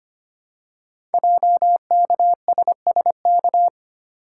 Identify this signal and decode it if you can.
Morse code, heard as a keyed tone, about 25 wpm, 700 Hz. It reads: JKSSK
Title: Code Groups - 5 Letter:Numbers - 25wpm